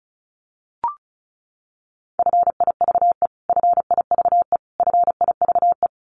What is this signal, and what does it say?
Morse code, heard as a keyed tone, about 34 wpm, 700 Hz. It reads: FIVE FIVE FIVE
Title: Top 500 Words Repeated 35wpm